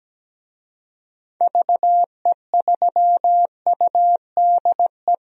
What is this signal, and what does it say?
Morse code, heard as a keyed tone, about 17 wpm, 700 Hz. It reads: VE3UDE
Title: DXCC-100 Call Signs - 6 Characters - 17wpm